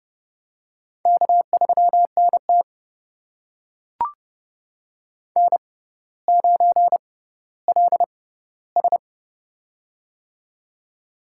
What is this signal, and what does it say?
Morse code, heard as a keyed tone, about 32 wpm, spaced as for 16 wpm, 700 Hz. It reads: K3NT N9LS
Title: USA Call Signs - 6x Character Spacing - 30wpm